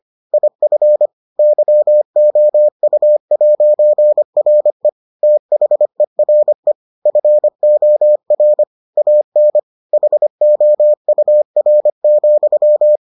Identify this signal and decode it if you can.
Morse code, heard as a keyed tone, about 25 wpm, 600 Hz. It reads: IF YOU'RE THERE FOR AN HOUR,